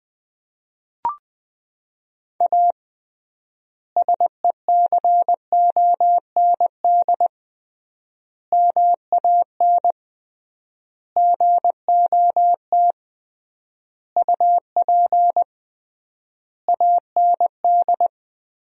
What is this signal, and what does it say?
Morse code, heard as a keyed tone, about 20 wpm, 700 Hz. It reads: A SECOND MAN GOT UP AND
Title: 3x Word-Spacing - Sentences from Top 300 Words - 20wpm